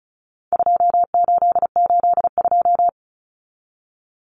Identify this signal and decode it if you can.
Morse code, heard as a keyed tone, about 34 wpm, 700 Hz. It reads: 2882
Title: Common Cut Numbers - 4-Digit 35wpm